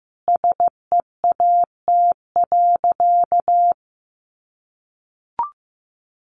Morse code, heard as a keyed tone, about 15 wpm, 700 Hz. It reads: SEAT.
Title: Sentences from Top 1500 Words 15wpm